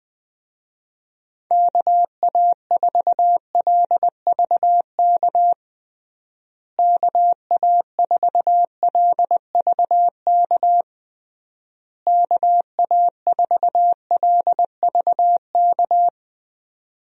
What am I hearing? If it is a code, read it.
Morse code, heard as a keyed tone, about 20 wpm, 700 Hz. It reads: KA4LVK KA4LVK KA4LVK
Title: USA Call Signs - 6 Characters - Repeated 3x - 20wpm